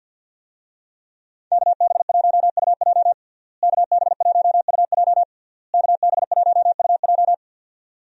Morse code, heard as a keyed tone, about 51 wpm, 700 Hz. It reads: KB1UJ KB1UJ KB1UJ
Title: USA Call Signs - 5 Characters - Repeated 3x - 50wpm